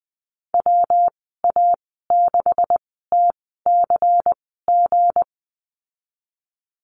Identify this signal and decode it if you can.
Morse code, heard as a keyed tone, about 20 wpm, spaced as for 14 wpm, 700 Hz. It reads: WA6TCG
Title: 179 Lesson 29g - Slash - Call Signs 20wpm